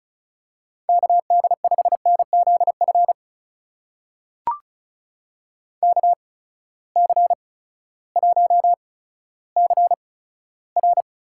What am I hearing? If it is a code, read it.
Morse code, heard as a keyed tone, about 36 wpm, spaced as for 16 wpm, 700 Hz. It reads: KD5NZF KC1CR
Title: USA Call Signs - 6 Characters - 8x Character Spacing - 35wpm